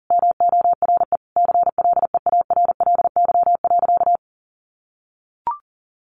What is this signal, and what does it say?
Morse code, heard as a keyed tone, about 40 wpm, 700 Hz. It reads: MORE CLEARLY.
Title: Sentences from Top 1300 Words 40wpm